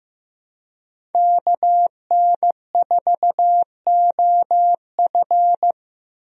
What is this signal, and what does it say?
Morse code, heard as a keyed tone, about 15 wpm, 700 Hz. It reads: KN4OF
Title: USA Call Signs - 5 Characters - 15wpm